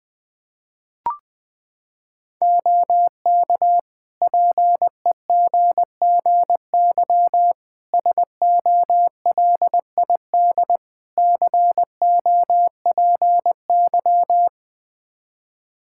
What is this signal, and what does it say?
Morse code, heard as a keyed tone, about 21 wpm, 700 Hz. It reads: OK PEGGY SOLID COPY
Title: Practice QSO - Exchange 3 - More Info - 20wpm